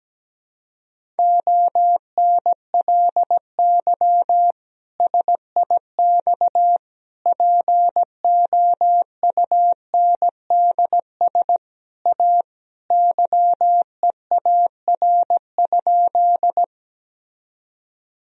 Morse code, heard as a keyed tone, about 17 wpm, 700 Hz. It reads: ONLY SIX POUNDS A YEAR?
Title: Sentences from Top 1000 Words - Review Part 2 - 17wpm